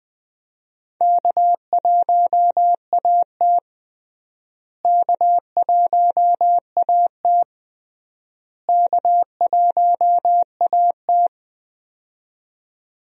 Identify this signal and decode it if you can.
Morse code, heard as a keyed tone, about 21 wpm, 700 Hz. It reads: K1AT K1AT K1AT
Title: USA Call Signs - 4 Characters - Repeated 3x - 20wpm